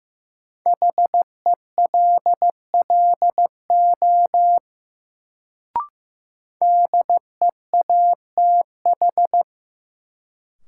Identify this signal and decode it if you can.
Morse code, heard as a keyed tone, about 15 wpm, 700 Hz. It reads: HELLO DEATH
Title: Sentences from Top 1100 Words - New Word Drill - 15wpm